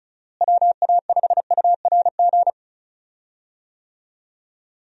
Morse code, heard as a keyed tone, about 36 wpm, 700 Hz. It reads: WA5URG
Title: USA Call Signs - 6 Characters - Repeated 2x - 35wpm